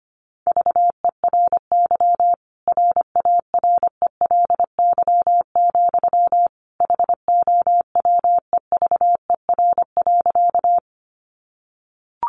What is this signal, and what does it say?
Morse code, heard as a keyed tone, about 25 wpm, 700 Hz. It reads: VERY RARELY, HOWEVER.
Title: Sentences from Top 1400 Words 25wpm